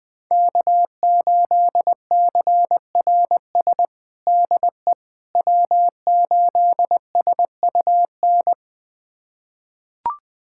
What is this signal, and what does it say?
Morse code, heard as a keyed tone, about 20 wpm, 700 Hz. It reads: K8CRS DE W8SUN
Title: USA Callsign DE Callsign - 5 Characters - 20wpm